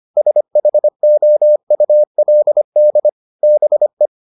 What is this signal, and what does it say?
Morse code, heard as a keyed tone, about 25 wpm, 600 Hz. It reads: SHOULD BE